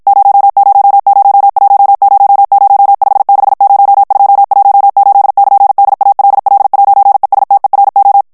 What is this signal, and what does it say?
Morse code, heard as a keyed tone, about 54 wpm, 800 Hz. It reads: 000100560218<KN>DALL'ESTERO
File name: morse mp3 / ITALIEN texte clair 1 vitesse 50 mots minute